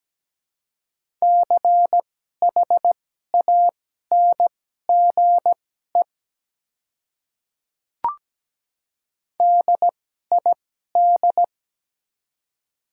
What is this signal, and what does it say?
Morse code, heard as a keyed tone, about 17 wpm, spaced as for 12 wpm, 700 Hz. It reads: CHANGE DID
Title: 2x Character Spacing - Top 300 Words - 17wpm